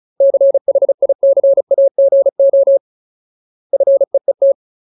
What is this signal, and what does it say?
Morse code, heard as a keyed tone, about 34 wpm, 550 Hz. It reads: CHICAGO FEET
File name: Ultimate Word List / 35WPM Ultimate